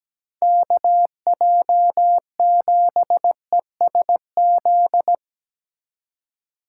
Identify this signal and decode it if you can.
Morse code, heard as a keyed tone, about 17 wpm, 700 Hz. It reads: KJ7ESZ
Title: USA Call Signs - 2x Character Spacing - 17wpm